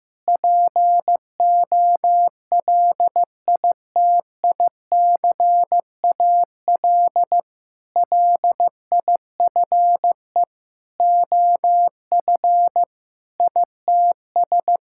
Morse code, heard as a keyed tone, about 15 wpm, 700 Hz. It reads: POLITICAL LIFE OF ITS
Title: Sentences from Top 1000 Words - Review Part 5 - 15wpm